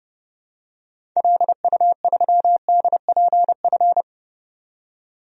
Morse code, heard as a keyed tone, about 31 wpm, 700 Hz. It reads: LU3DPF
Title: DXCC-100 Call Signs - 6 Characters - 30wpm